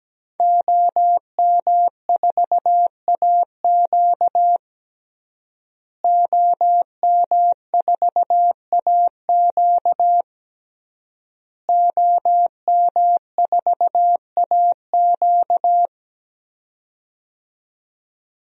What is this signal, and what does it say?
Morse code, heard as a keyed tone, about 17 wpm, 700 Hz. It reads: OM4AQ OM4AQ OM4AQ
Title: DXCC-100 Call Signs - 5 Characters - Repeated 3x - 17wpm